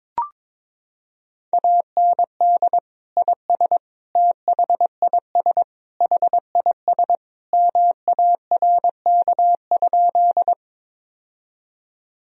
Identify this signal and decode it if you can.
Morse code, heard as a keyed tone, about 22 wpm, 700 Hz. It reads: AND IS THIS HIS MARK?
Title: Sentences from Top 1000 Words - Review Part 1 - 22wpm